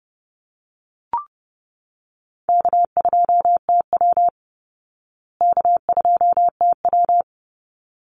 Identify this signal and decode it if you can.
Morse code, heard as a keyed tone, about 30 wpm, 700 Hz. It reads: K2TW K2TW
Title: Easy Call Signs Repeated 2x - 30wpm